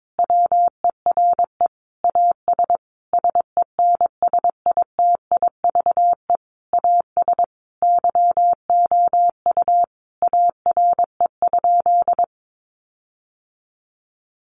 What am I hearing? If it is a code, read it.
Morse code, heard as a keyed tone, about 22 wpm, 700 Hz. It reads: WERE AS SENSITIVE AS YOU ARE?
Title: Sentences from Top 1400 Words 22wpm